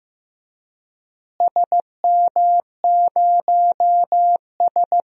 Morse code, heard as a keyed tone, about 16 wpm, 700 Hz. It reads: SM0S
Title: DXCC-100 Call Signs - 4 Characters - Repeated 2x - 15wpm